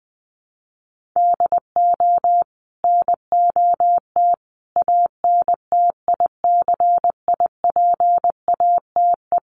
Morse code, heard as a keyed tone, about 20 wpm, 700 Hz. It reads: DO NOT ANTICIPATE
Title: Sentences from Top 1500 Words 20wpm